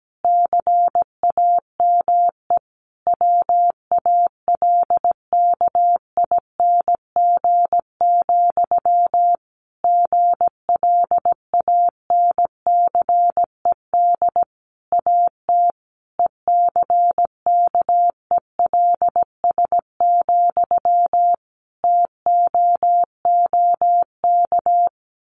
Morse code, heard as a keyed tone, about 17 wpm, 700 Hz. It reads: CAME WALKING, GLANCED AT ECKELS, TOOK
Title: A Sound Of Thunder 17wpm